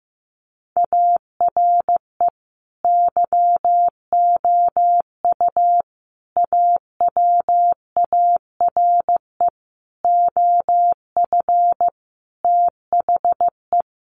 Morse code, heard as a keyed tone, about 15 wpm, 700 Hz. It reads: ARE YOU AWARE OF THE
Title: Sentences from Top 1500 Words 15wpm